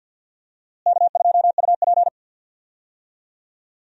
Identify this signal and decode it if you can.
Morse code, heard as a keyed tone, about 46 wpm, 700 Hz. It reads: K2UP
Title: USA Call Signs - 4 Characters - Repeated 3x - 50wpm